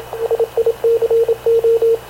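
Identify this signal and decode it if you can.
Morse code, heard as a keyed tone, about 26 wpm, 460 Hz, with radio noise. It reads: SICO
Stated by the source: Havana Radio CLA press (PX) broadcast, recorded by Radio Officer Paul Dumesnil aboard mv CNAV QUEST/CZDO.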